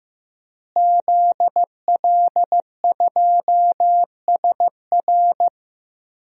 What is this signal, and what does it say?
Morse code, heard as a keyed tone, about 15 wpm, 700 Hz. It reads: ZL2SR
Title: DXCC-100 Call Signs - 5 Characters - Repeated 3x - 15wpm